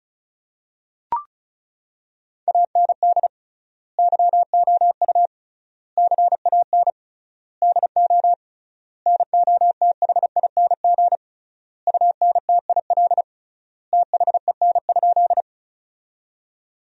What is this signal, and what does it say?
Morse code, heard as a keyed tone, about 36 wpm, 700 Hz. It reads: AND YOU CAN DO NOTHING UNTIL THEN?
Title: 3x Word-Spacing - Sentences from Top 500 Words - 35wpm